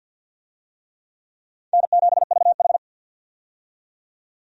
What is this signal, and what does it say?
Morse code, heard as a keyed tone, about 52 wpm, 700 Hz. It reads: N7VH
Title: USA Call Signs - 4 Characters - 50wpm